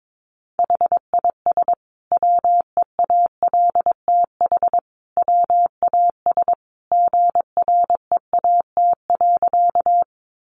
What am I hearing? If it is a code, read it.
Morse code, heard as a keyed tone, about 22 wpm, 700 Hz. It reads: HIS WEALTH WAS GREAT.
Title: Sentences from Top 1200 Words 22wpm